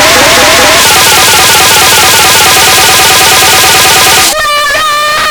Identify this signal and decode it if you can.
There is no code.